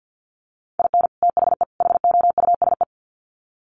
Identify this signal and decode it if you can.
Morse code, heard as a keyed tone, about 50 wpm, 700 Hz. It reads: IN THE HOUSE
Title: Sets of 3 Words 50wpm